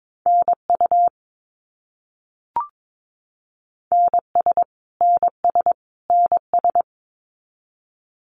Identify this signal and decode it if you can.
Morse code, heard as a keyed tone, about 22 wpm, 700 Hz. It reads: NU NS NS NS
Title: Canadian Provinces and Territories - Repeated 3x - 22wpm